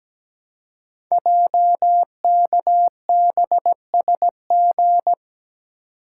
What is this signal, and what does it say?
Morse code, heard as a keyed tone, about 17 wpm, 700 Hz. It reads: JKBSG
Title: Code Groups - 5 Letters - 17wpm